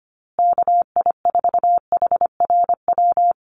Morse code, heard as a keyed tone, about 25 wpm, 700 Hz. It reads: KI4HRW
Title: USA Call Signs Repeated 3x 25wpm